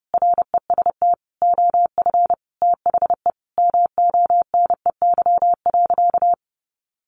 Morse code, heard as a keyed tone, about 30 wpm, 700 Hz. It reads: REST OF THE MONEY.
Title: Sentences from Top 1500 Words 30wpm